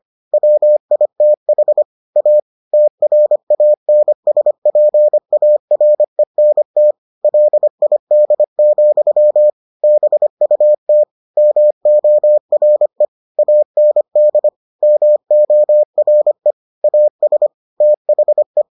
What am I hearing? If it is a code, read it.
Morse code, heard as a keyed tone, about 25 wpm, 600 Hz. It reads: WITH A TRANSPARENT LID, BUT MORE AND MORE AS THE